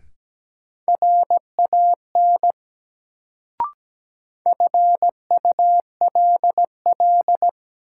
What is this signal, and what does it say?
Morse code, heard as a keyed tone, about 17 wpm, 700 Hz. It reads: RAN FULL
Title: Sentences from Top 500 Words - New Word Drill - 17wpm